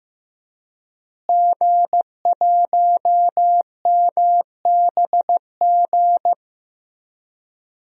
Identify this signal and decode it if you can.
Morse code, heard as a keyed tone, about 15 wpm, 700 Hz. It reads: G1MBG
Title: DXCC-100 Call Signs - 5 Characters - Repeated 3x - 15wpm